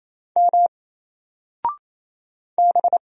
Morse code, heard as a keyed tone, about 28 wpm, 700 Hz. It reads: M B
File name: Single-Letter-Number-28wpm